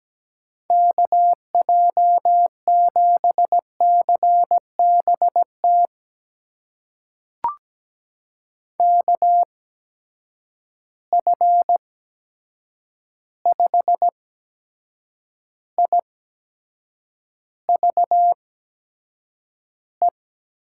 Morse code, heard as a keyed tone, about 17 wpm, spaced as for 7 wpm, 700 Hz. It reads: KJ7CBT KF5IVE
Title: USA Call Signs - 6 Characters - 8x Character Spacing - 17wpm